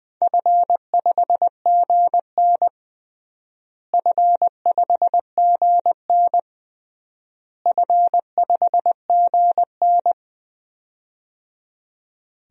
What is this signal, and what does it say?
Morse code, heard as a keyed tone, about 20 wpm, 700 Hz. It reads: F5GN F5GN F5GN
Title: DXCC-100 Call Signs - 4 Characters - Repeated 3x - 20wpm